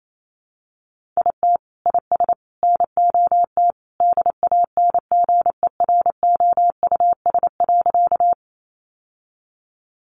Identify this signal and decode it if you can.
Morse code, heard as a keyed tone, about 28 wpm, 700 Hz. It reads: IT IS NOT DANGEROUS.
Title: Sentences from Top 1200 Words 28wpm